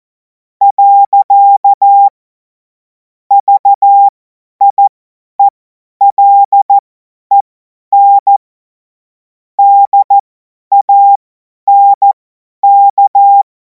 Morse code, heard as a keyed tone, about 14 wpm, spaced as for 10 wpm, 800 Hz. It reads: . VIELEN DANK
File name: Morsekurs / Witze / Witz 17. März 2026 50 F70